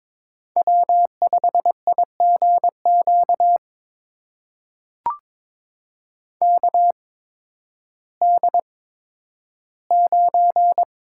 Morse code, heard as a keyed tone, about 22 wpm, spaced as for 12 wpm, 700 Hz. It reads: W5IGQ KD9
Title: USA Call Signs - 5 Characters - 8x Character Spacing - 22wpm